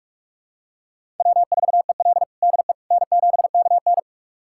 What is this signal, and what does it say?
Morse code, heard as a keyed tone, about 41 wpm, 700 Hz. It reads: W4EL DE N7KN
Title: USA Callsign DE Callsign - 4 Characters - 45wpm